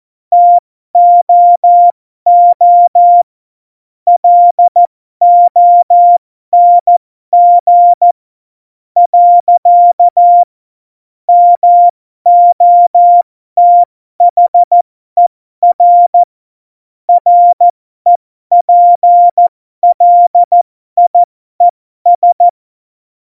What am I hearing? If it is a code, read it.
Morse code, heard as a keyed tone, about 14 wpm, spaced as for 12 wpm, 700 Hz. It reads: TOO LONG. MOTHER REPLIES
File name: Morsekurs / Witze